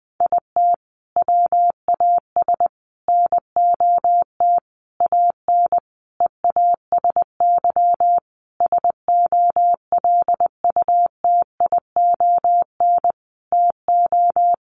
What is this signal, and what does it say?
Morse code, heard as a keyed tone, about 19 wpm, 700 Hz. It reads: IT WAS NOT AN EASY SOLUTION TO
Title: Sentences from Top 1300 Words 20wpm